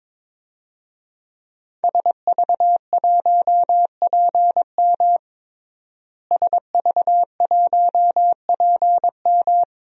Morse code, heard as a keyed tone, about 22 wpm, 700 Hz. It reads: SV1PM SV1PM
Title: DXCC-100 Call Signs - 5 Characters - Repeated 3x - 22wpm